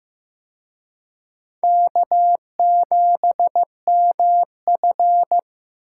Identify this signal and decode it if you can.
Morse code, heard as a keyed tone, about 18 wpm, 700 Hz. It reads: K7MF
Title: Speed Racing - USA 1x2 Call Signs 15wpm